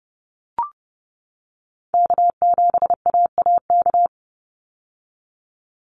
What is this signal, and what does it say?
Morse code, heard as a keyed tone, about 30 wpm, 700 Hz. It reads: K7AAK
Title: USA Call Signs 30wpm